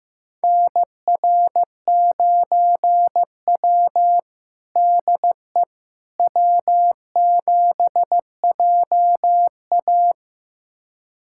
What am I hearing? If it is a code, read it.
Morse code, heard as a keyed tone, about 15 wpm, 700 Hz. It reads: NR9W DE W7JA
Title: USA Callsign DE Callsign - 4 Characters - 15wpm